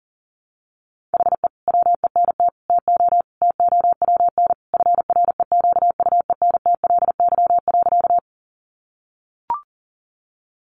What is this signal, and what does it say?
Morse code, heard as a keyed tone, about 37 wpm, 700 Hz. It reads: HE WENT TO TOWN FREQUENTLY.
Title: Sentences from Top 1300 Words 40wpm